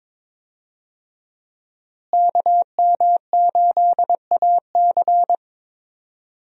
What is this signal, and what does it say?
Morse code, heard as a keyed tone, about 22 wpm, 700 Hz. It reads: KM8AC
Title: USA Call Signs - 5 Characters - Repeated 3x - 22wpm